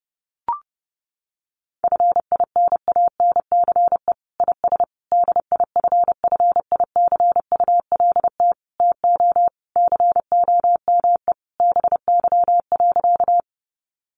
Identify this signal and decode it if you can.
Morse code, heard as a keyed tone, about 30 wpm, 700 Hz. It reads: FINANCE IS DIFFICULT TO COME BY.
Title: Sentences from Top 1500 Words 30wpm